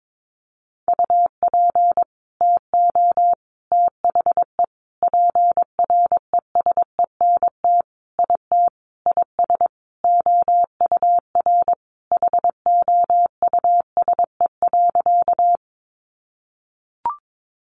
Morse code, heard as a keyed tone, about 22 wpm, 700 Hz. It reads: UP TO THE PRESENT IT IS OUR HOUSE.
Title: Sentences from Top 800 Words 22wpm